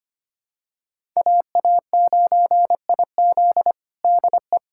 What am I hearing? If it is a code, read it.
Morse code, heard as a keyed tone, about 25 wpm, 700 Hz. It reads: AA9IZ DE
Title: USA Callsign DE Callsign - 5 Characters - 25wpm